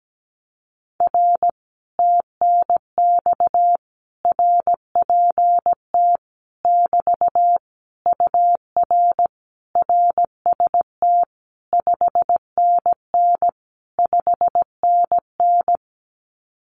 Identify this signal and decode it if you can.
Morse code, heard as a keyed tone, about 17 wpm, 700 Hz. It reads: R TNX RPT <BT> UR RST 5NN 5NN
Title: Common QSO Phrases Encore 17wpm